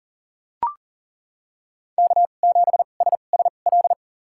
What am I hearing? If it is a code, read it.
Morse code, heard as a keyed tone, about 38 wpm, spaced as for 28 wpm, 700 Hz. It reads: K7SSL
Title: USA Call Signs - 5 Characters - 2x Character Spacing - 40wpm